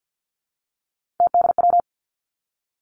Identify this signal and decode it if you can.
Morse code, heard as a keyed tone, about 50 wpm, 700 Hz. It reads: TDW
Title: Three Challenging Letter-Number Combinations 50wpm